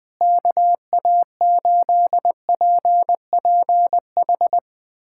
Morse code, heard as a keyed tone, about 21 wpm, 700 Hz. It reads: KA8PPH
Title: USA Callsign DE Callsign - 6 Characters - 20wpm